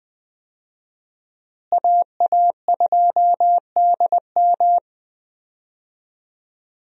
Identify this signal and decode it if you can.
Morse code, heard as a keyed tone, about 20 wpm, 700 Hz. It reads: AA2DM
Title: USA Call Signs - 5 Characters - Repeated 2x - 20wpm